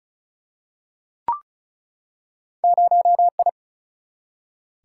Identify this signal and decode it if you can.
Morse code, heard as a keyed tone, about 36 wpm, 700 Hz. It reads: 0I
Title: Single-Letter Single-Number 35wpm